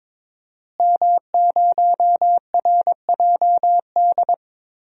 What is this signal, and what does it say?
Morse code, heard as a keyed tone, about 22 wpm, 700 Hz. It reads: M0RJD
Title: DXCC-100 Call Signs - 5 Characters - Repeated 3x - 22wpm